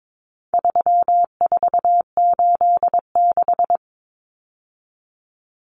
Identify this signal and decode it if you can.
Morse code, heard as a keyed tone, about 22 wpm, 700 Hz. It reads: 3486
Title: Common Cut Numbers - 4-Digit 22wpm